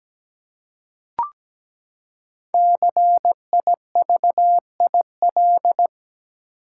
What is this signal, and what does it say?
Morse code, heard as a keyed tone, about 17 wpm, 700 Hz. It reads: CIVIL
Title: Five Letter Words - 17wpm